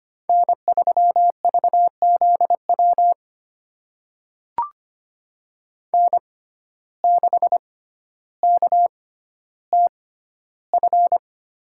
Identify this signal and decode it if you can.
Morse code, heard as a keyed tone, about 25 wpm, spaced as for 13 wpm, 700 Hz. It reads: N3VZW N6KTF
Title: USA Call Signs - 5 Characters - 6x Character Spacing - 25wpm